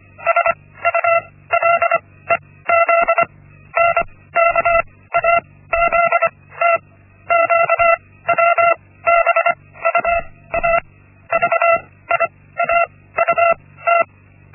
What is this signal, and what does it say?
Morse code, heard as a keyed tone, about 24 wpm, spaced as for 16 wpm, 680 Hz. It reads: SULEZ NKAZT QWBUA VIAUT
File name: Linksmakalnis-1